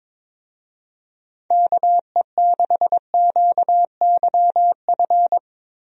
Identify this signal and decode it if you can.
Morse code, heard as a keyed tone, about 22 wpm, 700 Hz. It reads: KE6QYF
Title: USA Call Signs - 6 Characters - Repeated 3x - 22wpm